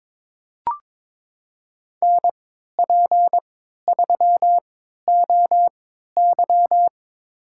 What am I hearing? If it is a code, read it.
Morse code, heard as a keyed tone, about 22 wpm, spaced as for 12 wpm, 700 Hz. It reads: NP3OY
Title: USA Call Signs - 3x Character Spacing - 22wpm